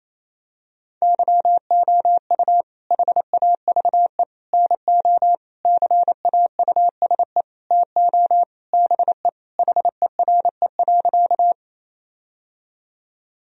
Morse code, heard as a keyed tone, about 28 wpm, 700 Hz. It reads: YOU HAVE NO CAUSE TO BE HERE.
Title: Sentences from Top 1000 Words - Review Part 1 - 28wpm